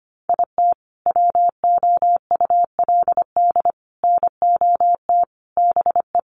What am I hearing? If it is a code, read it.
Morse code, heard as a keyed tone, about 25 wpm, 700 Hz. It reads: IT WOULD NOT BE
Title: Sentences from Top 1200 Words 25wpm